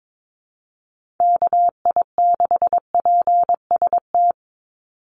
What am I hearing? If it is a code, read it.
Morse code, heard as a keyed tone, about 22 wpm, 700 Hz. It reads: KI6PST
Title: USA Call Signs 22wpm